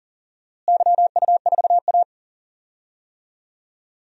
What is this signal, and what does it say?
Morse code, heard as a keyed tone, about 40 wpm, 700 Hz. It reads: YU4A
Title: DXCC-100 Call Signs - 4 Characters - Repeated 3x - 40wpm